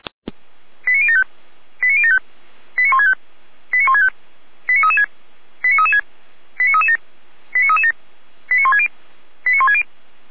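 Five-tone selective call (ZVEI), heard as quick runs of five tones, tones of about 70 ms. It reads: 89075 89075 89165 89165 89336 89336 89337 89337 89160 89160
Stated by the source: ZVEI five-tone selective call received by SDR, fire brigade Feuerwehr Orlamünde.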